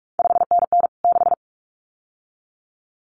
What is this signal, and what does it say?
Morse code, heard as a keyed tone, about 46 wpm, 700 Hz. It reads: 5NN 6
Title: CQ WW DX Contest Exchange Encore - 45wpm